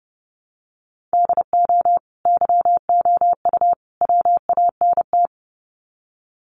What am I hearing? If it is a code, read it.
Morse code, heard as a keyed tone, about 30 wpm, 700 Hz. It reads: DO YOU WANT
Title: Sets of 3 Words 30wpm